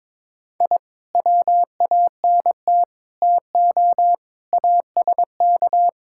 Morse code, heard as a keyed tone, about 22 wpm, 700 Hz. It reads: I WANT TO ASK
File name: practice / 22wpm / Build A Phrase - 5-Word Phrases - 22wpm